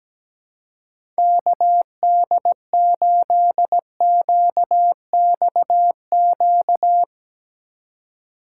Morse code, heard as a keyed tone, about 17 wpm, 700 Hz. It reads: KD8QXQ
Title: USA Call Signs - 4x Character Spacing - 17wpm